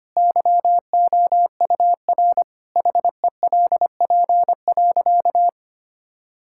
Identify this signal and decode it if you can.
Morse code, heard as a keyed tone, about 25 wpm, 700 Hz. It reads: YOUR HELP.
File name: practice / 25wpm / Sentences with 4-Word Phrases - 25wpm